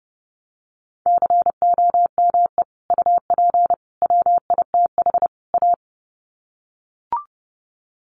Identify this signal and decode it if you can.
Morse code, heard as a keyed tone, about 30 wpm, 700 Hz. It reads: COME UP WITH A
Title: Sets of 4 Words 30wpm